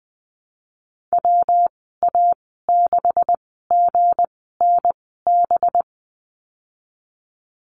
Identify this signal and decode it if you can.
Morse code, heard as a keyed tone, about 20 wpm, spaced as for 15 wpm, 700 Hz. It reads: WA6GNB
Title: 186 Lesson 30g - Letter b - Call Signs 20wpm